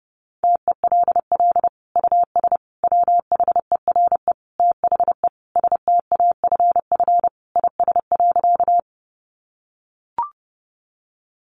Morse code, heard as a keyed tone, about 30 wpm, 700 Hz. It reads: TELL US WHERE THE STAFF IS.
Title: Sentences from Top 1200 Words 30wpm